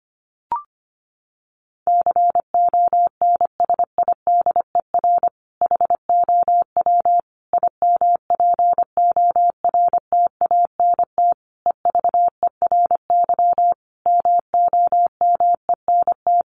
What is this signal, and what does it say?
Morse code, heard as a keyed tone, about 25 wpm, 700 Hz. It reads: CONSIDER HOW IMPORTANT EVERY MOMENT
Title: Sentences from Top 1000 Words 25wpm